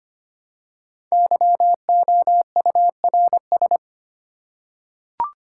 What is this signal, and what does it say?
Morse code, heard as a keyed tone, about 25 wpm, 700 Hz. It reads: YOURS
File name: 5 Letter Words - First 3 Letters Given - 25wpm